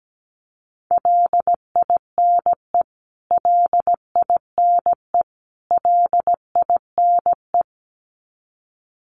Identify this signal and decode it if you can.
Morse code, heard as a keyed tone, about 17 wpm, 700 Hz. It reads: LINE LINE LINE
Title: Top 200 Words Repeated 17wpm